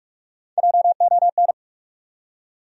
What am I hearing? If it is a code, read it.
Morse code, heard as a keyed tone, about 44 wpm, 700 Hz. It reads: JON
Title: USA Common Names - Max 4 Letters - 45wpm